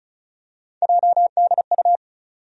Morse code, heard as a keyed tone, about 36 wpm, 700 Hz. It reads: JDU
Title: USA Call Sign Postfixes - 3 Letters 35wpm